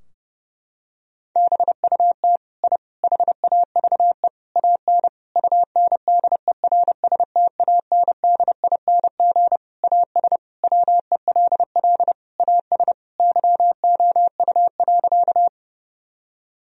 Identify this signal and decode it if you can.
Morse code, heard as a keyed tone, about 30 wpm, 700 Hz. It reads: BUT I HAVE AN UNDERSTANDING AS WELL AS YOU.
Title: Sentences from Top 1000 Words - Review Part 3 - 30wpm